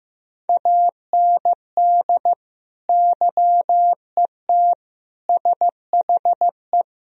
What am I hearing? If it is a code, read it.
Morse code, heard as a keyed tone, about 15 wpm, 700 Hz. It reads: AND YET SHE
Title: Sentences from Top 1000 Words - Review Part 3 - 15wpm